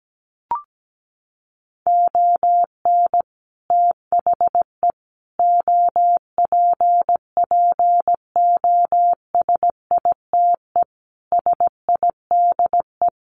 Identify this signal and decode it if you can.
Morse code, heard as a keyed tone, about 17 wpm, 700 Hz. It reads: ON THE OPPOSITE SIDE
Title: Sets of 5 Words Encore2 17wpm